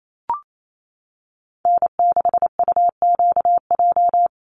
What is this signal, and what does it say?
Morse code, heard as a keyed tone, about 28 wpm, 700 Hz. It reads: N6UQJ
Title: USA Call Signs 28wpm